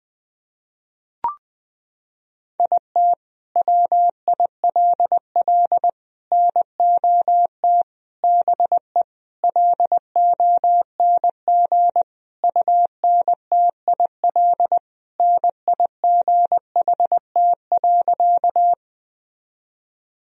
Morse code, heard as a keyed tone, about 21 wpm, 700 Hz. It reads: IT WILL NOT BE LONG UNTIL NIGHT.
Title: Sentences from Top 1000 Words - Review Part 1 - 20wpm